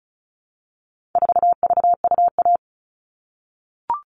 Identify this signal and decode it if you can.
Morse code, heard as a keyed tone, about 34 wpm, 700 Hz. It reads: 4VUA
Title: All Cut Numbers - 4-Digit 35wpm